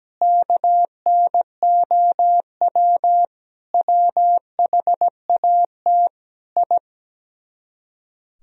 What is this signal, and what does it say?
Morse code, heard as a keyed tone, about 17 wpm, 700 Hz. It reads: KNOW WHAT I
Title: Speed-Racing - Sets of 5 Words 17wpm